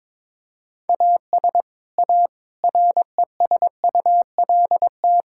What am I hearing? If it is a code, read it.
Morse code, heard as a keyed tone, about 22 wpm, 700 Hz. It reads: AS A RESULT
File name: practice / 22wpm / Build A Phrase - 5-Word Phrases - 22wpm